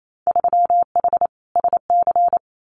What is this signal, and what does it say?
Morse code, heard as a keyed tone, about 28 wpm, 700 Hz. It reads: 3H SC
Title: Winter Field Day Exchanges 28wpm